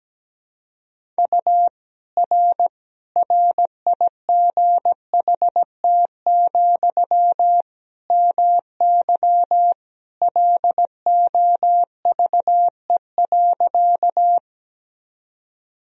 Morse code, heard as a keyed tone, about 17 wpm, 700 Hz. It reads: U R RIGHT, MY LOVE.
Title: Sentences from Top 400 Words and Abbreviations 17wpm